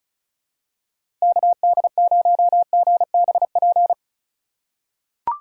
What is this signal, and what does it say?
Morse code, heard as a keyed tone, about 34 wpm, 700 Hz. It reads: KD0GBP
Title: USA Call Signs - 6 Characters - 2x Character Spacing - 35wpm